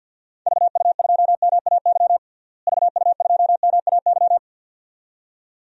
Morse code, heard as a keyed tone, about 50 wpm, 700 Hz. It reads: VU2MAY VU2MAY
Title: DXCC-100 Call Signs - 6 Characters - Repeated 2x - 50wpm